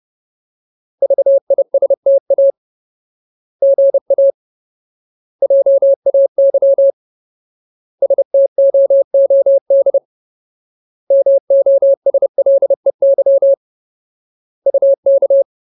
Morse code, heard as a keyed tone, about 30 wpm, 550 Hz. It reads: VISTA GA JAY STOOD MOSLEY UK
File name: Ultimate Word List / 30WPM Ultimate